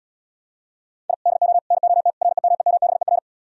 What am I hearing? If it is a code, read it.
Morse code, heard as a keyed tone, about 77 wpm, 700 Hz. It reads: I DO NOT REMEMBER